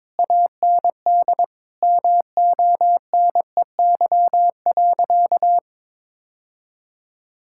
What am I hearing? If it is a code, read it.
Morse code, heard as a keyed tone, about 22 wpm, 700 Hz. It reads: AND MONEY.
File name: Difficult Sentences with Hints - 22wpm